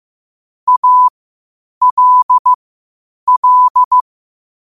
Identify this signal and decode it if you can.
Morse code, heard as a keyed tone, about 15 wpm, spaced as for 8 wpm, 1000 Hz. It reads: ALL
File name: qotdfiles / 8wpm